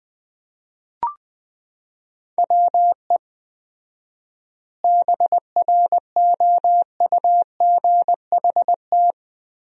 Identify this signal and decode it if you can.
Morse code, heard as a keyed tone, about 20 wpm, 700 Hz. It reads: WE BROUGHT
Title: 4x Word-Spacing - Sentences from Top 500 Words - 20wpm